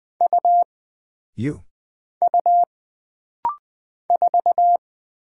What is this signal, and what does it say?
Morse code, heard as a keyed tone, about 20 wpm, 700 Hz. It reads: U U 4
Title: Single Character (Letter:Number:Punctuation:Prosign) - Rapid-Fire - 20wpm